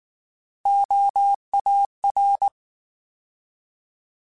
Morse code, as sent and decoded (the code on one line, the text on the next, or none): --- .- .-.
OAR